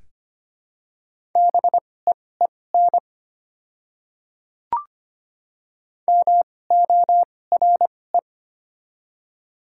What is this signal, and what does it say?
Morse code, heard as a keyed tone, about 25 wpm, spaced as for 18 wpm, 700 Hz. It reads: BEEN MORE
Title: 2x Character Spacing - Top 300 Words - 25wpm